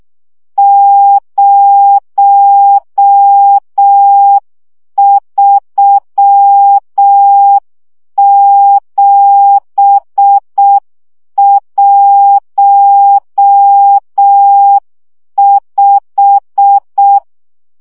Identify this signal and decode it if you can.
Morse code, heard as a keyed tone, about 6 wpm, 800 Hz. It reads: 03715
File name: morze vizsga 2011-2 5